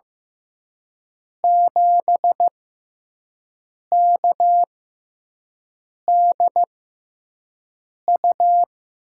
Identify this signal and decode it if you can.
Morse code, heard as a keyed tone, about 15 wpm, spaced as for 5 wpm, 700 Hz. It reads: 7KDU
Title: USA Call Signs - 5 Characters - 6x Character Spacing - 15wpm